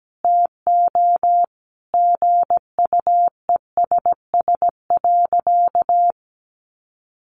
Morse code, heard as a keyed tone, about 17 wpm, 700 Hz. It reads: TO GUESS.
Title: Sentences from Top 1500 Words 17wpm